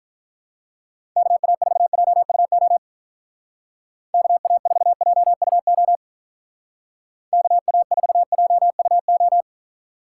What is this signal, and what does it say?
Morse code, heard as a keyed tone, about 42 wpm, 700 Hz. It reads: KA4JUO KA4JUO KA4JUO
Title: Speed Racing - USA 2x3 Call Signs 35wpm